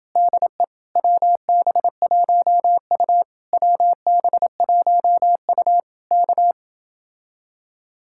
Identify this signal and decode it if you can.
Morse code, heard as a keyed tone, about 28 wpm, 700 Hz. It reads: DE WB1U WB1U K